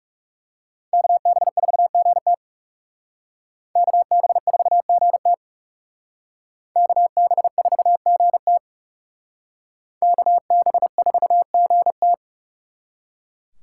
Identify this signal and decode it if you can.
Morse code, heard as a keyed tone, about 38 wpm, 700 Hz. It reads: KB4GT KB4GT KB4GT KB4GT
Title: Speed Racing - USA 2x2 Call Signs 30wpm